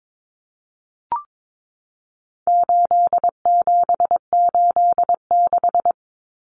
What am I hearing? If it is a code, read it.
Morse code, heard as a keyed tone, about 22 wpm, 700 Hz. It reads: 8786
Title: Numbers - 4-Digit 22wpm